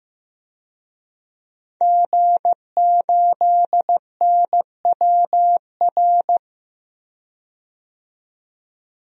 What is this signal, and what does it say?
Morse code, heard as a keyed tone, about 15 wpm, 700 Hz. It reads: G8NWR
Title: DXCC-100 Call Signs - 5 Characters - 15wpm